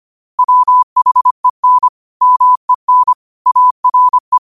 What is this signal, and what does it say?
Morse code, heard as a keyed tone, about 25 wpm, 1000 Hz. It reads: WHEN MEN ARE
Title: 25 WPM morse code quotes for Thu, 29 Feb 2024